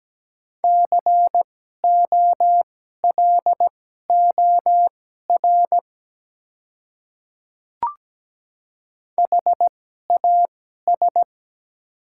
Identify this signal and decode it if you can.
Morse code, heard as a keyed tone, about 17 wpm, spaced as for 12 wpm, 690 Hz. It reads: COLOR HAS
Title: 2x Character Spacing - Top 300 Words - 17wpm